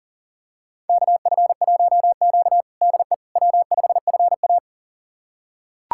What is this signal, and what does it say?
Morse code, heard as a keyed tone, about 40 wpm, 700 Hz. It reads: KF1Q DE W5FA
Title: USA Callsign DE Callsign - 4 Characters - 40wpm